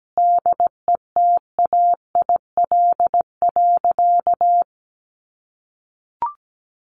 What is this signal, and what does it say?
Morse code, heard as a keyed tone, about 17 wpm, 700 Hz. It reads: DETAIL.
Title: Sentences from Top 1500 Words 17wpm